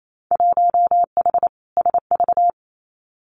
Morse code, heard as a keyed tone, about 28 wpm, 700 Hz. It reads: 1H SV
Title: Winter Field Day Exchanges 28wpm